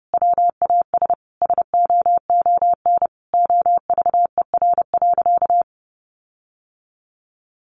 Morse code, heard as a keyed tone, about 30 wpm, 700 Hz. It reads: WAS SOON OVER.
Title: Sentences from Top 1300 Words 30wpm